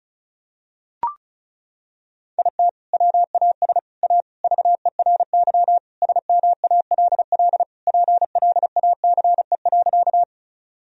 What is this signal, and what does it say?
Morse code, heard as a keyed tone, about 36 wpm, 700 Hz. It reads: IT WAS A VERY SMALL PLACE.
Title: Sentences from Top 1000 Words - Review Part 1 - 35wpm